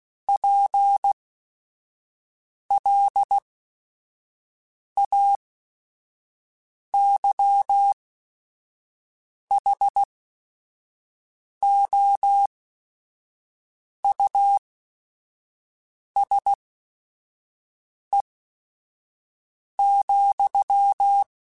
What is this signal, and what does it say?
Morse code, heard as a keyed tone, about 16 wpm, spaced as for 5 wpm, 800 Hz. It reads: PLAYHOUSE,
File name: General / General-mp3s / Houses @5